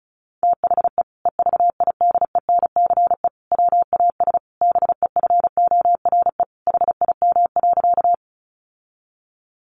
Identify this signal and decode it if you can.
Morse code, heard as a keyed tone, about 34 wpm, 700 Hz. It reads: THE EVIDENCE WAS BEFORE HIM.
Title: Sentences from Top 1300 Words 35wpm